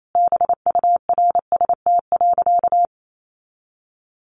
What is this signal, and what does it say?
Morse code, heard as a keyed tone, about 28 wpm, 700 Hz. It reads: BURST.
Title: A Sound Of Thunder 28wpm